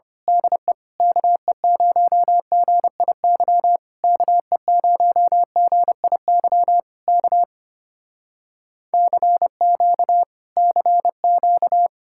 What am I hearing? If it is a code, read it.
Morse code, heard as a keyed tone, about 25 wpm, 700 Hz. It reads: DE KE0GIY KE0GIY K CQ CQ
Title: Speed-Racing Common QSO Phrases Encore - 25wpm